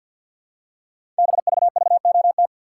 Morse code, heard as a keyed tone, about 52 wpm, 700 Hz. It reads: BVVOT
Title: Code Groups - 5 Letter:Numbers - 50wpm